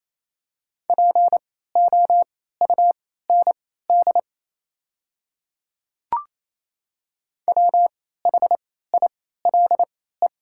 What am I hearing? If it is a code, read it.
Morse code, heard as a keyed tone, about 28 wpm, spaced as for 16 wpm, 700 Hz. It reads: POUND WHILE
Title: 3x Character Spacing - Top 500 Words - 28wpm